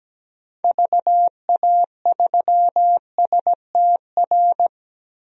Morse code, heard as a keyed tone, about 17 wpm, 700 Hz. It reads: VA3STR
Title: DXCC-100 Call Signs - 6 Characters - 17wpm